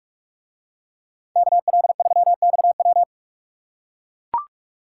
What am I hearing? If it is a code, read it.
Morse code, heard as a keyed tone, about 44 wpm, 700 Hz. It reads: KL3XW